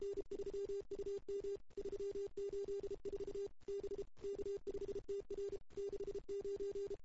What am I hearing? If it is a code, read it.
Morse code, heard as a keyed tone, about 31 wpm, 380 Hz. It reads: N3UM 384 B K5TR 69
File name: CW / 2008-ss-cw-k5tr-op-n5rz -00:00